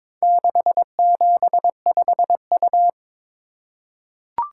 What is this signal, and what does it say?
Morse code, heard as a keyed tone, about 22 wpm, 700 Hz. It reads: 675U
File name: practice / 22wpm / Four Letter-Number Combinations - 22wpm